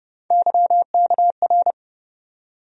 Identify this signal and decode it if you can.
Morse code, heard as a keyed tone, about 30 wpm, 700 Hz. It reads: YKR
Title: USA Call Sign Postfixes - 3 Letters 30wpm